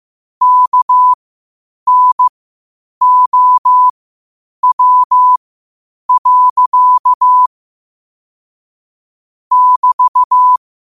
Morse code, heard as a keyed tone, about 15 wpm, spaced as for 8 wpm, 1000 Hz. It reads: KNOW. <BT>
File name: qotdfiles / 8wpm